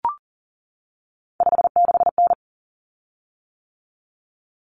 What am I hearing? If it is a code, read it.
Morse code, heard as a keyed tone, about 40 wpm, 700 Hz. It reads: 56N
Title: Common Cut Numbers - 3-Digit 40wpm